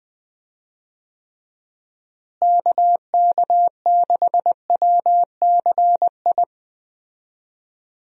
Morse code, heard as a keyed tone, about 20 wpm, 700 Hz. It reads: KK6WCI
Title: USA Call Signs - 6 Characters - Repeated 2x - 20wpm